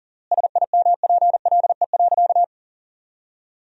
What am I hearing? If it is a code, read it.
Morse code, heard as a keyed tone, about 40 wpm, 700 Hz. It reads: SIMPLE.
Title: 3x Word-Spacing - Sentences from Top 500 Words - 40wpm